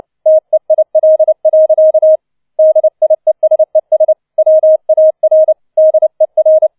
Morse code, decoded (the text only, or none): TEIL. DIESES WAR DER